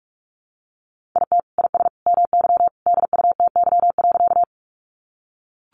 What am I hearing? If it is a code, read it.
Morse code, heard as a keyed tone, about 44 wpm, 700 Hz. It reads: IT IS MY DUTY.
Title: Sentences from Top 1500 Words 45wpm